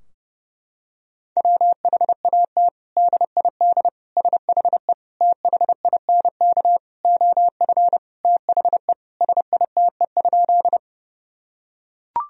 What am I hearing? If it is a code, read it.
Morse code, heard as a keyed tone, about 32 wpm, 700 Hz. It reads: WHAT DID SHE THINK OF THE SITE?
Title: Sentences from Top 1000 Words - Review Part 4 - 30wpm